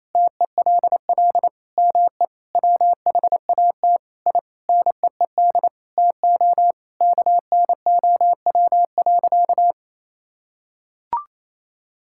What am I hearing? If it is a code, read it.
Morse code, heard as a keyed tone, about 27 wpm, 700 Hz. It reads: TELL ME WHAT I NEED TO KNOW.
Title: Sentences from Top 1000 Words - Review Part 5 - 28wpm